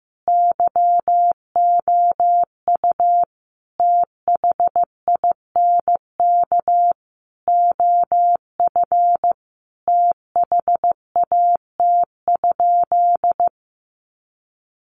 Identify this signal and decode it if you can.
Morse code, heard as a keyed tone, about 15 wpm, 700 Hz. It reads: YOU THINK OF THAT?
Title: Sentences from Top 200 Words 15wpm